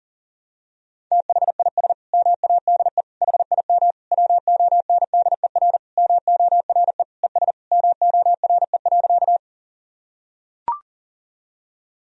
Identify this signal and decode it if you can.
Morse code, heard as a keyed tone, about 37 wpm, 700 Hz. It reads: THIS MADE HIM WONDER MORE ES MORE.
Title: Sentences from Top 500 Words and Abbreviations 40wpm